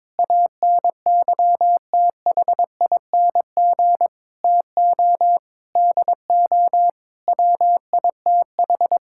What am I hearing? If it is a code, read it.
Morse code, heard as a keyed tone, about 21 wpm, 700 Hz. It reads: ANYTHING TO DO WITH
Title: Sentences from Top 1000 Words - Review Part 5 - 22wpm